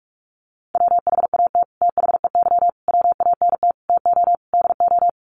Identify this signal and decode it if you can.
Morse code, heard as a keyed tone, about 44 wpm, 700 Hz. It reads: WHAT THEY WANT TO DO
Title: Sets of 5 Words Encore2 45wpm